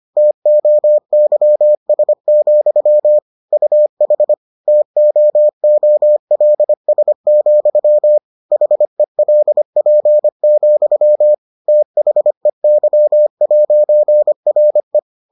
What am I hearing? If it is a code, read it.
Morse code, heard as a keyed tone, about 25 wpm, 600 Hz. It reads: TOYS, UH TOOLS, HELP, THEY'RE